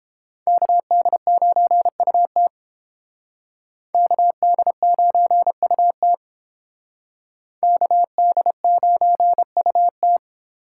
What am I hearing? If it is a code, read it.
Morse code, heard as a keyed tone, about 29 wpm, 700 Hz. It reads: KD9UT KD9UT KD9UT